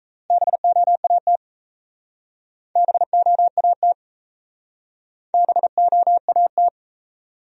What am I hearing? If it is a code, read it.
Morse code, heard as a keyed tone, about 36 wpm, 700 Hz. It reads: BOAT BOAT BOAT